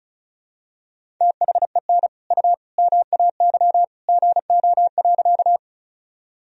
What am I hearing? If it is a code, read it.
Morse code, heard as a keyed tone, about 34 wpm, 700 Hz. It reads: THEN U MAY GO.
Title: Sentences from Top 100 Words and Abbreviations 35wpm